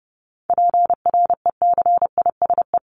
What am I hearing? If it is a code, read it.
Morse code, heard as a keyed tone, about 30 wpm, 700 Hz. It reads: PRECISE
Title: LY Suffix - Top 100 Words with Base Words 30wpm